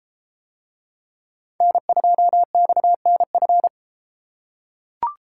Morse code, heard as a keyed tone, about 30 wpm, 700 Hz. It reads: N2XNF